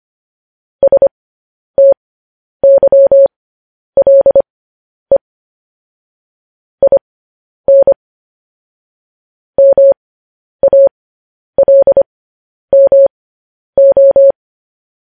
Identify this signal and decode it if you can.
Morse code, heard as a keyed tone, about 25 wpm, spaced as for 10 wpm, 550 Hz. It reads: STYLE IN MALMO
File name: podcast / News headlines-1x-25x10